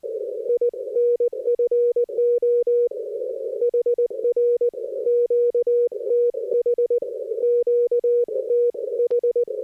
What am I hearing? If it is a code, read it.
Morse code, heard as a keyed tone, about 19 wpm, 480 Hz. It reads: INFO HR QTH QTH
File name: 60m qso